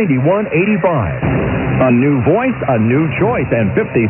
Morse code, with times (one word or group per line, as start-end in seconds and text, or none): none